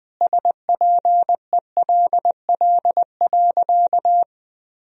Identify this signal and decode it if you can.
Morse code, heard as a keyed tone, about 20 wpm, 700 Hz. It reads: SPELL.
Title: Sentences from Top 1000 Words - Review Part 1 - 20wpm